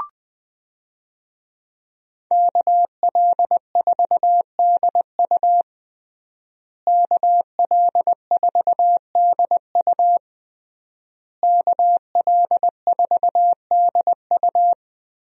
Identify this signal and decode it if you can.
Morse code, heard as a keyed tone, about 20 wpm, 700 Hz. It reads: KL4DU KL4DU KL4DU
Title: USA Call Signs - 5 Characters - Repeated 3x - 20wpm